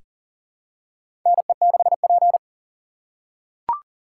Morse code, heard as a keyed tone, about 40 wpm, 700 Hz. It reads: NE6P
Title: USA Call Signs - 4 Characters - 8x Character Spacing - 40wpm